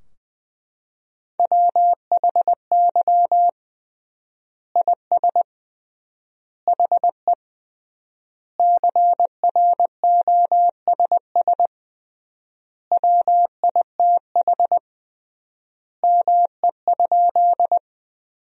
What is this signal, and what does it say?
Morse code, heard as a keyed tone, about 20 wpm, 700 Hz. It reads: WHY IS HE CROSS WITH ME?
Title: 3x Word-Spacing - Sentences from Top 300 Words - 20wpm